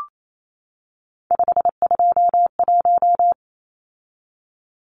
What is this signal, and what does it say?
Morse code, heard as a keyed tone, about 28 wpm, 700 Hz. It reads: 521
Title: Numbers - 3-Digit 28wpm